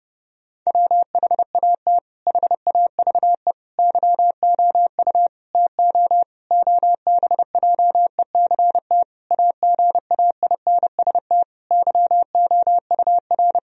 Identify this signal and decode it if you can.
Morse code, heard as a keyed tone, about 30 wpm, 700 Hz. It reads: WHAT HAVE YOU TO OBJECT AGAINST YOUR
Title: Sentences from Top 1000 Words - Review Part 3 - 30wpm